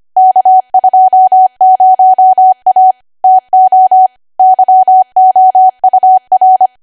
Morse code, heard as a keyed tone, about 25 wpm, 750 Hz. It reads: K20A TO YOUR
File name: Archive / 25 WPM / 25WPM